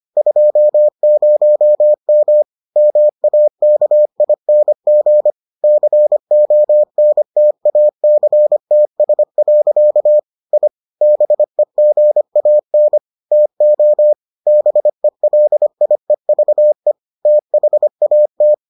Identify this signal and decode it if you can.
Morse code, heard as a keyed tone, about 25 wpm, 600 Hz. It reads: 20M MAKING CONTACTS. I BEGAN TO BELIEVE THAT